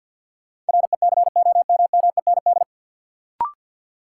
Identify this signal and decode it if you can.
Morse code, heard as a keyed tone, about 47 wpm, 700 Hz. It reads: RECOMMEND
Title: Sentences from Top 1200 Words - New Word Drill Alt - 50wpm